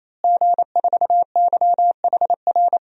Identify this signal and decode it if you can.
Morse code, heard as a keyed tone, about 28 wpm, 700 Hz. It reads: G4YHR
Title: DXCC-100 Call Signs - 5 Characters - Repeated 3x - 28wpm